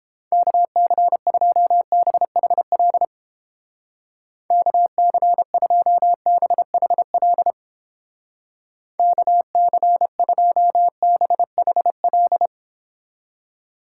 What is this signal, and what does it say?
Morse code, heard as a keyed tone, about 31 wpm, 700 Hz. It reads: KC2BHL KC2BHL KC2BHL